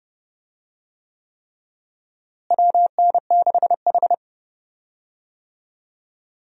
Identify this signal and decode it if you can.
Morse code, heard as a keyed tone, about 30 wpm, 700 Hz. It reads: WN6H
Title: USA Call Signs - 4 Characters - Repeated 3x - 30wpm